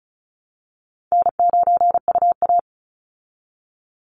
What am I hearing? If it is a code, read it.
Morse code, heard as a keyed tone, about 34 wpm, 700 Hz. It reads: N9UA
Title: Easy Call Signs Repeated 2x - 35wpm